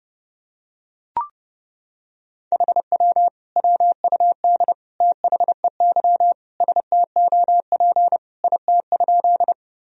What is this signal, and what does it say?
Morse code, heard as a keyed tone, about 32 wpm, 700 Hz. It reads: HW WUD THEY STOP IT?
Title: Sentences from Top 300 Words and Abbreviations 30wpm